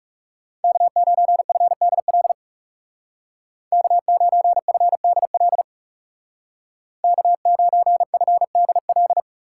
Morse code, heard as a keyed tone, about 35 wpm, 700 Hz. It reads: K9FDL K9FDL K9FDL
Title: Speed Racing - USA 1x3 Call Signs 30wpm